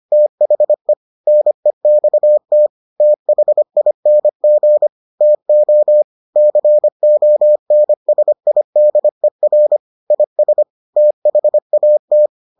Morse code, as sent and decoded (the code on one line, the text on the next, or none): - .... / -. -..- - / - .... .. -. --. / - --- / -.-. --- -. ... .. -.. .-. / .. ... / - .... .- -
THE NEXT THING TO CONSIDER IS THAT